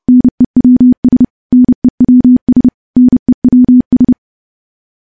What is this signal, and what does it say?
Morse code, heard as a keyed tone, about 30 wpm, 250 Hz. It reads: NEWS NEWS NEWS
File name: morse / morse-training-30WPM